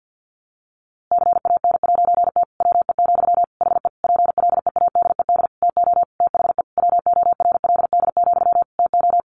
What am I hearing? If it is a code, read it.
Morse code, heard as a keyed tone, about 50 wpm, 700 Hz. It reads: CAN'T WE, HE PLEADED TO THE WORLD, TO
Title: A Sound Of Thunder 50wpm